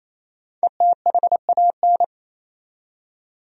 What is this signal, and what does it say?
Morse code, heard as a keyed tone, about 28 wpm, 700 Hz. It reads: ETHAN
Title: USA Common Names - Max 5 Letters - 28wpm